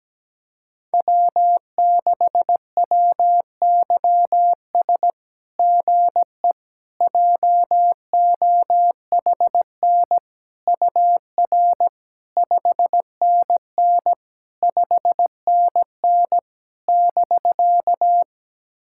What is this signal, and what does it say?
Morse code, heard as a keyed tone, about 17 wpm, 700 Hz. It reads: W6WYS GE JOHN UR 5NN 5NN <BK>
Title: SOTA QSO Practice 17wpm